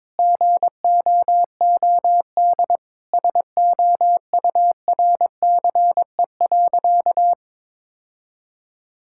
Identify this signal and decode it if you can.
Morse code, heard as a keyed tone, about 22 wpm, 700 Hz. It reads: GOOD SOURCE.
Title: Sentences from Top 1000 Words - Review Part 5 - 22wpm